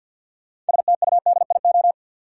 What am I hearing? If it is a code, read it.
Morse code, heard as a keyed tone, about 52 wpm, 700 Hz. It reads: STUDIO
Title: Sentences from Top 1200 Words - New Word Drill Alt - 50wpm